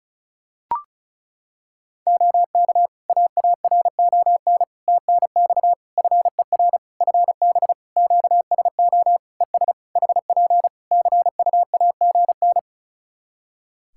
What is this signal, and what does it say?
Morse code, heard as a keyed tone, about 36 wpm, 700 Hz. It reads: OK AARON TNX FER FB QSO ES HP CUAGN
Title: K1USN SST QSO Practice 35wpm